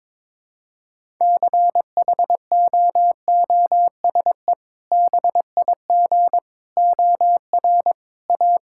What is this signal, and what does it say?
Morse code, heard as a keyed tone, about 22 wpm, 700 Hz. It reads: CHOOSE BIG OR A
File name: practice / 22wpm / Sentences with 2-Word Phrases - 22wpm